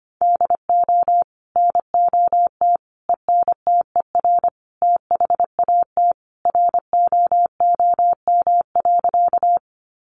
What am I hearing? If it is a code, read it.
Morse code, heard as a keyed tone, about 25 wpm, 700 Hz. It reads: DO NOT ENTER THAT ROOM.
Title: Sentences from Top 1300 Words 25wpm